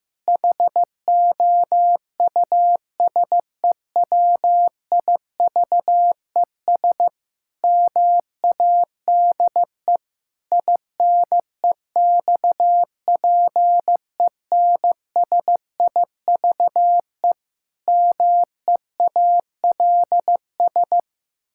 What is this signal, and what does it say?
Morse code, heard as a keyed tone, about 16 wpm, 700 Hz. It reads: HOUSEWIVES MADE INEXPENSIVE MEALS
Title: Pangram Challenge v2 - 15wpm